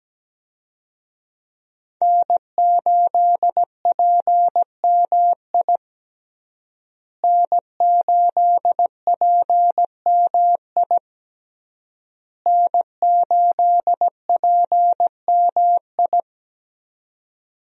Morse code, heard as a keyed tone, about 17 wpm, 700 Hz. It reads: N8PMI N8PMI N8PMI
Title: USA Call Signs - 5 Characters - Repeated 3x - 17wpm